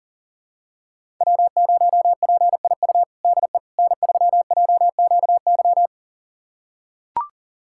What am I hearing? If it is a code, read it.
Morse code, heard as a keyed tone, about 40 wpm, 700 Hz. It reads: W0PIU DE N3JQY
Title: USA Callsign DE Callsign - 5 Characters - 40wpm